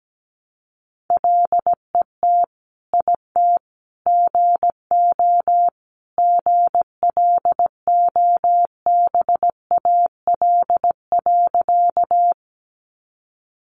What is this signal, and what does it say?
Morse code, heard as a keyed tone, about 17 wpm, 700 Hz. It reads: LET IT GO GLOBAL.
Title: Sentences from Top 1200 Words 17wpm